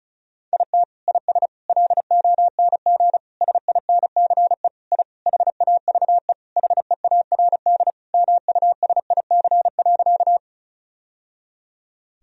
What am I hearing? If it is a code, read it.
Morse code, heard as a keyed tone, about 36 wpm, 700 Hz. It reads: IT IS LONG SINCE I HAVE HEARD MUSIC.
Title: Sentences from Top 1000 Words - Review Part 1 - 35wpm